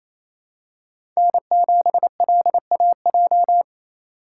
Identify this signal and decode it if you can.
Morse code, heard as a keyed tone, about 28 wpm, 700 Hz. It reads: N7LAJ
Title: USA Call Signs - 5 Characters - 2x Character Spacing - 28wpm